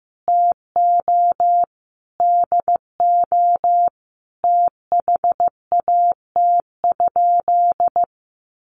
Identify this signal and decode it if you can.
Morse code, heard as a keyed tone, about 15 wpm, 700 Hz. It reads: TO DO THAT?
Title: Sentences from Top 900 Words 15wpm